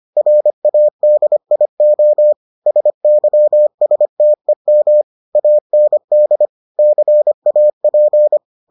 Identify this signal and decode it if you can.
Morse code, heard as a keyed tone, about 25 wpm, 600 Hz. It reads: RADIO SYSTEM AND CAP